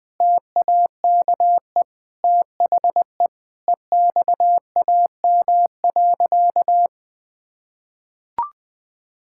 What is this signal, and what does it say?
Morse code, heard as a keyed tone, about 20 wpm, 700 Hz. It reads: TAKE THE EXAM.
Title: Sentences from Top 1000 Words - Review Part 5 - 20wpm